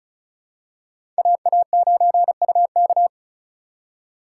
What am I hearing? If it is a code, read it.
Morse code, heard as a keyed tone, about 34 wpm, 700 Hz. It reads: AA9UK
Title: USA Call Signs - 5 Characters - 4x Character Spacing - 35wpm